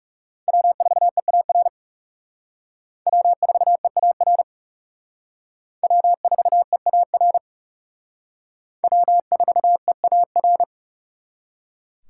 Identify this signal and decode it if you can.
Morse code, heard as a keyed tone, about 37 wpm, 700 Hz. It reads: W4EAR W4EAR W4EAR W4EAR
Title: Speed Racing - USA 1x3 Call Signs 30wpm